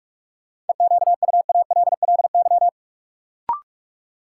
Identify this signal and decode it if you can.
Morse code, heard as a keyed tone, about 46 wpm, 700 Hz. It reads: EQUALLY
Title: Sentences from Top 1400 Words - New Word Drill - 45wpm